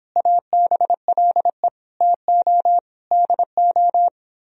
Morse code, heard as a keyed tone, about 31 wpm, 700 Hz. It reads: ABLE TO DO